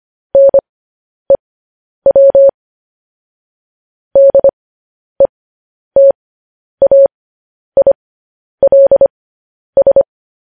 Morse code, heard as a keyed tone, about 25 wpm, spaced as for 10 wpm, 550 Hz. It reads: NEW DETAILS
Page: news headlines in morse code at 10 wpm